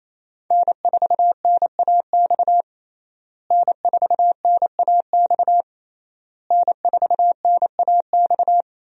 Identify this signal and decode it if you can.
Morse code, heard as a keyed tone, about 28 wpm, 700 Hz. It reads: N4NAX N4NAX N4NAX
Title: USA Call Signs - 5 Characters - Repeated 3x - 28wpm